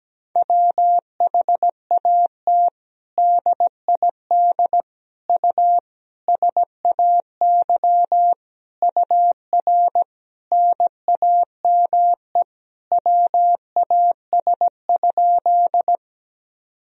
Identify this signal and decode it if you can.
Morse code, heard as a keyed tone, about 17 wpm, 700 Hz. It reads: WHAT DID U SAY UR NAME WAS?
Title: Sentences from Top 200 Words and Abbreviations 17wpm